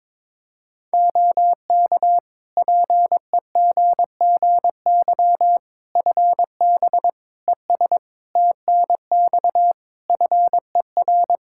Morse code, heard as a keyed tone, about 21 wpm, 700 Hz. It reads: OK PEGGY FB ES TNX FER
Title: Practice QSO - Exchange 2 - Info - 22wpm